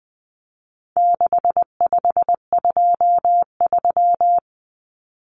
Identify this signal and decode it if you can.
Morse code, heard as a keyed tone, about 20 wpm, 700 Hz. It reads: 6523
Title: Numbers - 4-Digit 20wpm